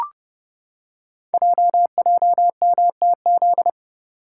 Morse code, heard as a keyed tone, about 30 wpm, 700 Hz. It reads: JJMTZ
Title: Code Groups - 5 Letters - 30wpm